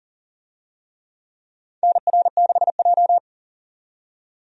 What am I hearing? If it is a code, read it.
Morse code, heard as a keyed tone, about 40 wpm, 700 Hz. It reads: NR6J
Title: USA Call Signs - 4 Characters - 40wpm